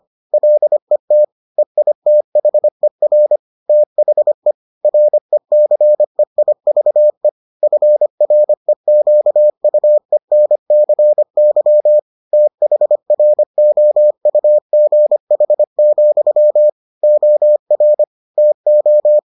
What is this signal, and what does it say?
Morse code, heard as a keyed tone, about 25 wpm, 600 Hz. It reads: LET EITHER THE RECEIVE FREQUENCY THROUGH, OR TO